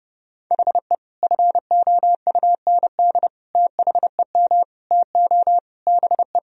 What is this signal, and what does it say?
Morse code, heard as a keyed tone, about 32 wpm, 700 Hz. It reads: HE FOUND THEM TO BE
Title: Sentences from Top 1000 Words - Review Part 3 - 30wpm